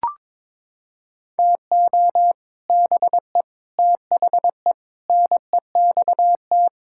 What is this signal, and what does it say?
Morse code, heard as a keyed tone, about 22 wpm, 700 Hz. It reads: TO BE THE NEXT
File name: Sets of 9 Words - 22wpm